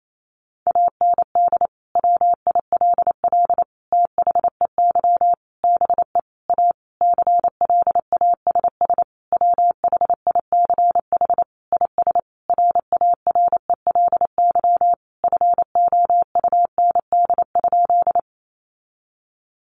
Morse code, heard as a keyed tone, about 28 wpm, 700 Hz. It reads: AND WILL THEY BE A CLASS WHICH IS RARELY FOUND?
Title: Sentences from Top 1400 Words 28wpm